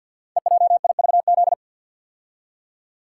Morse code, heard as a keyed tone, about 49 wpm, 700 Hz. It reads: EJIVZ
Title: Code Groups - 5 Letters - 50wpm